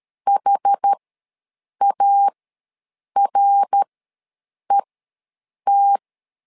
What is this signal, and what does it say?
Morse code, heard as a keyed tone, about 13 wpm, spaced as for 7 wpm, 800 Hz. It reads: HARET
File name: LEZIONI / 3 MWRO / 3 MWR ws 12wpm 1809